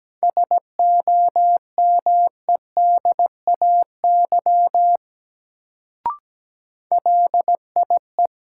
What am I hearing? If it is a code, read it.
Morse code, heard as a keyed tone, about 17 wpm, 700 Hz. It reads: SOMEDAY LIE
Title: Sentences from Top 1600 Words - New Word Drill - 17wpm